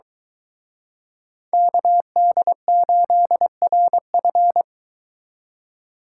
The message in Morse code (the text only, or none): KD8RF